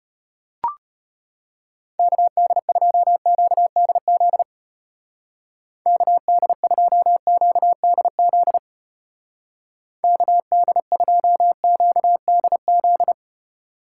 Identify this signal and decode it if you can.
Morse code, heard as a keyed tone, about 33 wpm, 700 Hz. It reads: KD2QDZ KD2QDZ KD2QDZ